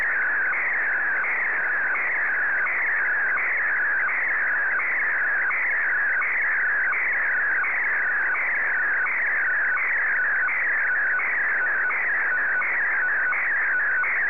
SSTV (slow-scan television), PD 90.
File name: sstv / W2180-VK8VTX